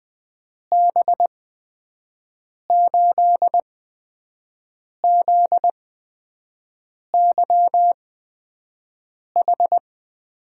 Morse code, heard as a keyed tone, about 21 wpm, spaced as for 5 wpm, 700 Hz. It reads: B8ZYH
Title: USA Call Signs - 6 Characters - 8x Character Spacing - 20wpm